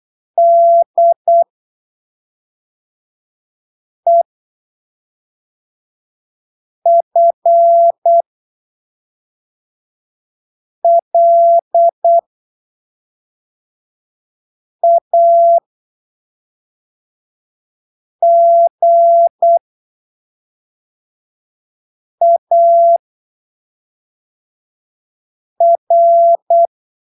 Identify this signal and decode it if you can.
Morse code, heard as a keyed tone, about 8 wpm, spaced as for 3 wpm, 680 Hz. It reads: DEFLAGAR
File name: arquivos / texto / texto 30 08ppm